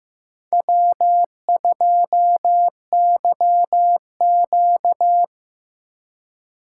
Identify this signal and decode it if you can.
Morse code, heard as a keyed tone, about 18 wpm, 700 Hz. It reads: W2YQ
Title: Speed Racing - USA 1x2 Call Signs 15wpm